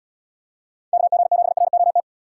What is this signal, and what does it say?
Morse code, heard as a keyed tone, about 78 wpm, 700 Hz. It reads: KG7ROT